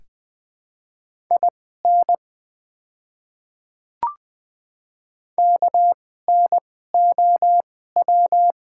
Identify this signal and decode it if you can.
Morse code, heard as a keyed tone, about 20 wpm, spaced as for 15 wpm, 700 Hz. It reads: IN KNOW
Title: 2x Character Spacing - Top 100 Words - 20wpm